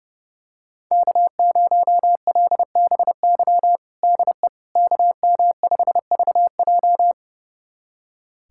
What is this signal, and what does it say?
Morse code, heard as a keyed tone, about 30 wpm, 700 Hz. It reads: K0LBY DE KM5VJ
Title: USA Callsign DE Callsign - 5 Characters - 30wpm